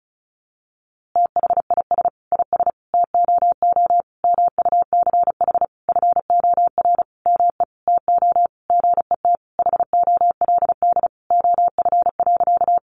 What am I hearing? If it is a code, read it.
Morse code, heard as a keyed tone, about 34 wpm, 700 Hz. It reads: THIS IS TOO MUCH FOR ME TO GET HOLD OF.
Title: A Sound Of Thunder 35wpm